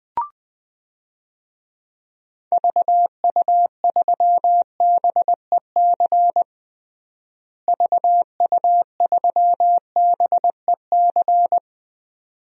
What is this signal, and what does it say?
Morse code, heard as a keyed tone, about 21 wpm, 700 Hz. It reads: VU3BEC VU3BEC
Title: DXCC-100 Call Signs - 6 Characters - Repeated 2x - 20wpm